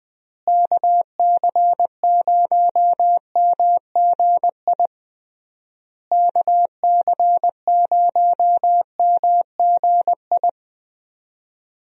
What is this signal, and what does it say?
Morse code, heard as a keyed tone, about 21 wpm, 700 Hz. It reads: KC0MGI KC0MGI
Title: USA Call Signs - 6 Characters - Repeated 3x - 20wpm